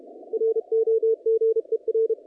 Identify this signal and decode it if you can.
Morse code, heard as a keyed tone, about 34 wpm, 470 Hz, with radio noise. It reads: ROGER
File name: contests / naqp-cw-k5tr-op-k5pi- -0300